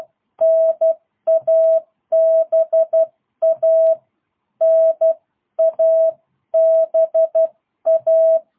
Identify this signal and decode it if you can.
Morse code, heard as a keyed tone, about 12 wpm, 650 Hz. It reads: NABA NABA
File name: jh8cbh / 2022cw